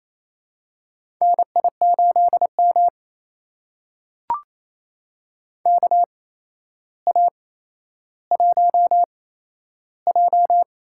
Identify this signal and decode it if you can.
Morse code, heard as a keyed tone, about 28 wpm, spaced as for 12 wpm, 700 Hz. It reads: NI8M KA1J
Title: USA Call Signs - 8x Character Spacing - 28wpm